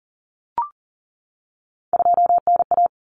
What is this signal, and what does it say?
Morse code, heard as a keyed tone, about 40 wpm, 700 Hz. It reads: 2NA
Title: Common Cut Numbers - 3-Digit 40wpm